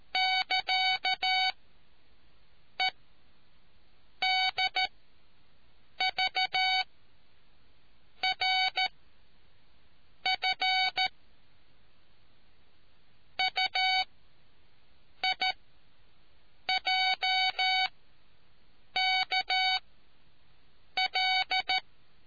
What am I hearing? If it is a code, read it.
Morse code, heard as a keyed tone, about 13 wpm, 750 Hz. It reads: <KA> E D V R F U I J K L